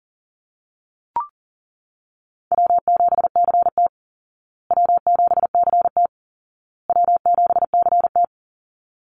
Morse code, heard as a keyed tone, about 40 wpm, 700 Hz. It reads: W7CT W7CT W7CT
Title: Easy Call Signs Repeated 3x - 40wpm